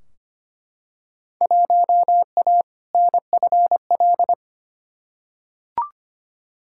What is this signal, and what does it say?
Morse code, heard as a keyed tone, about 25 wpm, 700 Hz. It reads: 1A NFL
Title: Field Day Exchanges - 25wpm